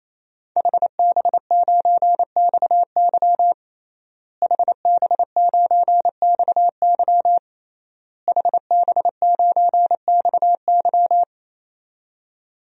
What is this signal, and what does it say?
Morse code, heard as a keyed tone, about 28 wpm, 700 Hz. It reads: HB9XY HB9XY HB9XY
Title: DXCC-100 Call Signs - 5 Characters - Repeated 3x - 28wpm